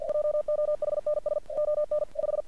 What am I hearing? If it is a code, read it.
Morse code, heard as a keyed tone, about 47 wpm, 600 Hz. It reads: JOHNSON 5
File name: archived / CRSnet